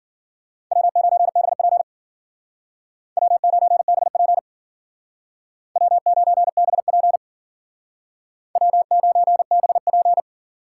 Morse code, heard as a keyed tone, about 52 wpm, 700 Hz. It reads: W9BP W9BP W9BP W9BP